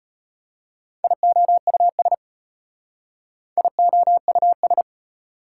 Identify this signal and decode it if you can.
Morse code, heard as a keyed tone, about 33 wpm, 700 Hz. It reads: IOUS IOUS